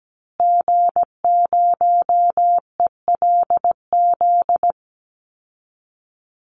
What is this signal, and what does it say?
Morse code, heard as a keyed tone, about 17 wpm, 700 Hz. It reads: G0ELZ
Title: Call Signs Repeated 3x - 17wpm